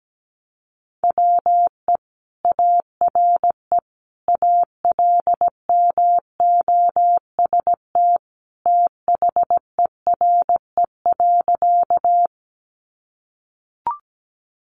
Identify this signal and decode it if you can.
Morse code, heard as a keyed tone, about 17 wpm, 700 Hz. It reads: WE ARE ALMOST THERE.
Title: Sentences from Top 700 Words 17wpm